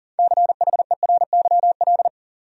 Morse code, heard as a keyed tone, about 40 wpm, 700 Hz. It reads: CHERYL
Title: USA Common Names - 40wpm